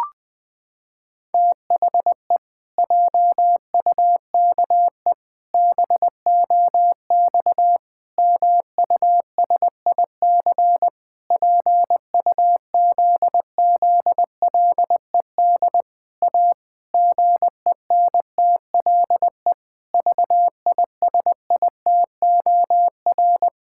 Morse code, heard as a keyed tone, about 20 wpm, 700 Hz. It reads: THE JUKE BOX MUSIC PUZZLED A GENTLE VISITOR
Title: Pangram Challenge v2 - 20wpm